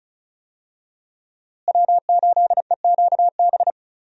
Morse code, heard as a keyed tone, about 34 wpm, 700 Hz. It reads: W8EQB
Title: USA Call Signs - 5 Characters - Repeated 2x - 35wpm